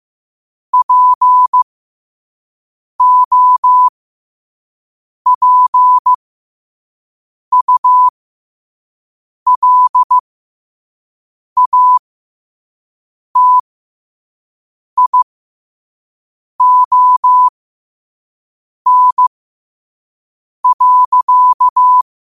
Morse code, heard as a keyed tone, about 15 wpm, spaced as for 5 wpm, 1000 Hz. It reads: POPULATION.
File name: qotdfiles / 5wpm